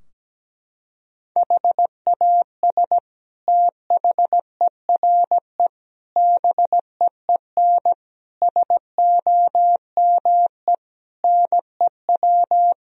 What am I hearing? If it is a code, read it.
Morse code, heard as a keyed tone, about 17 wpm, 700 Hz. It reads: HAS THERE BEEN SOME NEW
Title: Sentences from Top 1000 Words - Review Part 4 - 17wpm